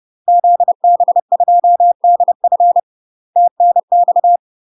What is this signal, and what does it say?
Morse code, heard as a keyed tone, about 30 wpm, 700 Hz. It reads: ZB2DF TNX
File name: UK MASTER QSOs TRAINING / 30WPM / UK MASTER QSOs 0027